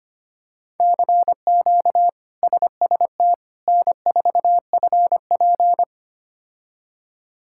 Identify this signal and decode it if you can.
Morse code, heard as a keyed tone, about 25 wpm, 700 Hz. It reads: CQ SST N4FP
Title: K1USN SST QSO Practice - 25wpm